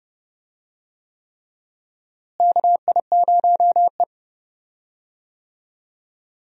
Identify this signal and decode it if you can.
Morse code, heard as a keyed tone, about 31 wpm, 700 Hz. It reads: KI0E
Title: USA Call Signs - 4 Characters - Repeated 3x - 30wpm